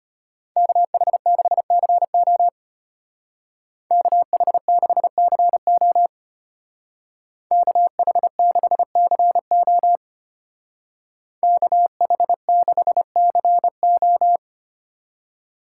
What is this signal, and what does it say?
Morse code, heard as a keyed tone, about 30 wpm, 700 Hz. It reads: KH6CO KH6CO KH6CO KH6CO
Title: Speed Racing - USA 2x2 Call Signs 25wpm